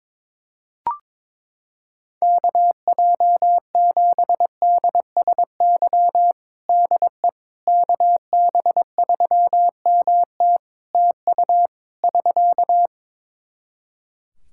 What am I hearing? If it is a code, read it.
Morse code, heard as a keyed tone, about 22 wpm, 700 Hz. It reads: KJ7DSY DE KB3MT TU <SK>
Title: K1USN SST QSO Practice 22wpm